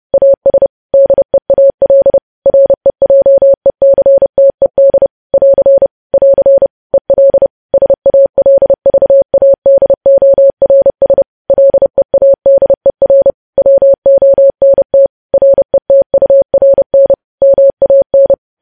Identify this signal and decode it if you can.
Morse code, heard as a keyed tone, about 30 wpm, 550 Hz. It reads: AS DEAL REJECTED <AR> <AR> EL SALVADORS LEADER WONT RETURN MAN
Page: news headlines in morse code at 30 wpm